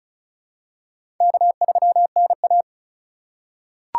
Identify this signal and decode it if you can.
Morse code, heard as a keyed tone, about 36 wpm, 700 Hz. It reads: K3NA
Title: USA Call Signs - 4 Characters - 3x Character Spacing - 35wpm